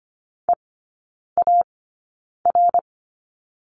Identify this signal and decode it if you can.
Morse code, heard as a keyed tone, about 25 wpm, spaced as for 9 wpm, 690 Hz. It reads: EAR
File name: Lesson 15e - Number 5 - Three Characters 25wpm